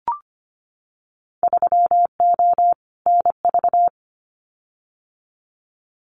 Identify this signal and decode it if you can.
Morse code, heard as a keyed tone, about 25 wpm, 700 Hz. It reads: 3O NV
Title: Winter Field Day Exchanges Encore 25wpm